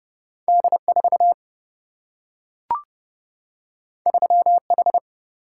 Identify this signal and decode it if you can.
Morse code, heard as a keyed tone, about 29 wpm, 700 Hz. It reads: D4 3H
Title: Single-Letter Single-Number 30wpm